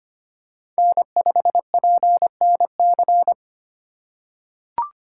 Morse code, heard as a keyed tone, about 25 wpm, 700 Hz. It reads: N5PNC
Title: USA Call Signs - 4x Character Spacing - 25wpm